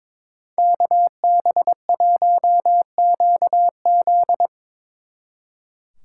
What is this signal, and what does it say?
Morse code, heard as a keyed tone, about 25 wpm, 700 Hz. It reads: KB1QZ
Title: Speed Racing - USA Call Signs 22wpm